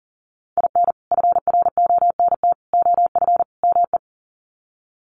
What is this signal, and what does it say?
Morse code, heard as a keyed tone, about 37 wpm, 700 Hz. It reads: IN FRONT OF ME
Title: Sets of 4 Words 40wpm